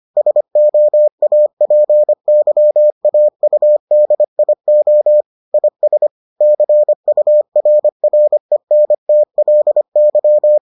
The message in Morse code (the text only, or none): SOAPYAUDIO IS CURRENTLY